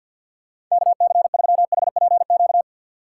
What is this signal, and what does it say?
Morse code, heard as a keyed tone, about 50 wpm, 700 Hz. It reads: KC3HPQ
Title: USA Call Signs - 4x Character Spacing - 50wpm